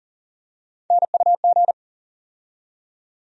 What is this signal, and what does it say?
Morse code, heard as a keyed tone, about 40 wpm, 700 Hz. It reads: NUG
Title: USA Call Sign Postfixes - 3 Letters 40wpm